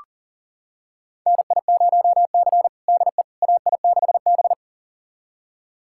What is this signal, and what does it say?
Morse code, heard as a keyed tone, about 40 wpm, 700 Hz. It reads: NI0C DE AI6B
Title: USA Callsign DE Callsign - 4 Characters - 40wpm